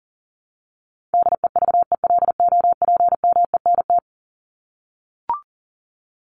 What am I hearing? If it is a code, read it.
Morse code, heard as a keyed tone, about 37 wpm, 700 Hz. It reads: DEVELOPMENT
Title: Top 501-750 Words 40wpm